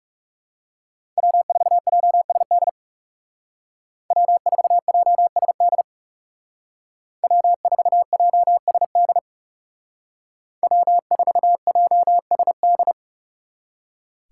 Morse code, heard as a keyed tone, about 39 wpm, 700 Hz. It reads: W4JSD W4JSD W4JSD W4JSD